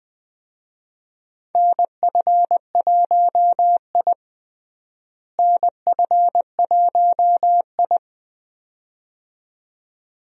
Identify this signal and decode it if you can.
Morse code, heard as a keyed tone, about 20 wpm, 700 Hz. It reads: NF1I NF1I
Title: USA Call Signs - 4 Characters - Repeated 2x - 20wpm